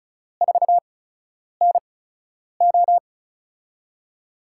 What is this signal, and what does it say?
Morse code, heard as a keyed tone, about 34 wpm, spaced as for 9 wpm, 700 Hz. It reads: 4NO
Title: USA Call Signs - 5 Characters - 8x Character Spacing - 35wpm